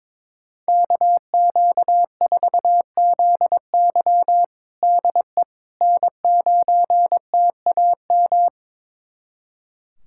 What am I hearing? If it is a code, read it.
Morse code, heard as a keyed tone, about 22 wpm, 700 Hz. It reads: KQ4ZY DE N9TAM
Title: USA Callsign DE Callsign - 5 Characters - 22wpm